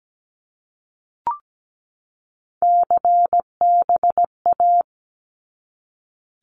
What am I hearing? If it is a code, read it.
Morse code, heard as a keyed tone, about 17 wpm, 700 Hz. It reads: CBA
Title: Abbreviations 17wpm